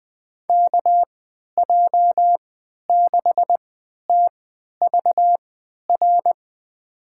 Morse code, heard as a keyed tone, about 20 wpm, spaced as for 11 wpm, 700 Hz. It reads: KJ6TVR
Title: USA Call Signs - 6 Characters - 3x Character Spacing - 20wpm